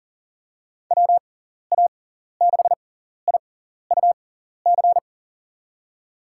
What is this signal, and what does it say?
Morse code, heard as a keyed tone, about 40 wpm, spaced as for 14 wpm, 700 Hz. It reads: WA6IUC
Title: USA Call Signs - 6 Characters - 6x Character Spacing - 40wpm